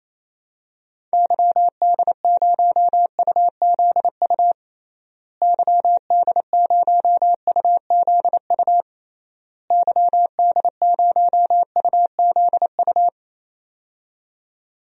Morse code, heard as a keyed tone, about 27 wpm, 700 Hz. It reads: YD0UZU YD0UZU YD0UZU
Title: DXCC-100 Call Signs - 6 Characters - Repeated 3x - 28wpm